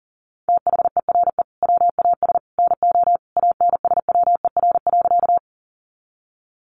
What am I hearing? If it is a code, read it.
Morse code, heard as a keyed tone, about 40 wpm, 700 Hz. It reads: THERE WAS NO ANSWER.
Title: Sentences from Top 300 Words 40wpm